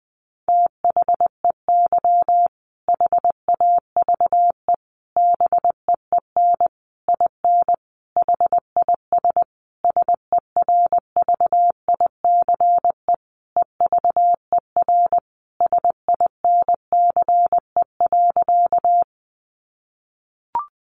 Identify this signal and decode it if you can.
Morse code, heard as a keyed tone, about 20 wpm, 700 Hz. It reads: THEY HAVE BEEN IN HIS SERVICE EVER SINCE.
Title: Sentences from Top 700 Words 20wpm